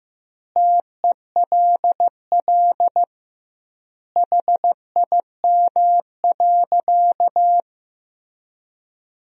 Morse code, heard as a keyed tone, about 15 wpm, 700 Hz. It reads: TELL HIM.
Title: 2x Word-Spacing - Sentences from Top 200 Words - 15wpm